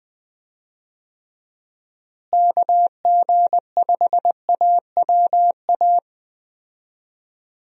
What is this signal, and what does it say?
Morse code, heard as a keyed tone, about 21 wpm, 700 Hz. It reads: KG5AWA
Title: USA Call Signs - 6 Characters - Repeated 2x - 20wpm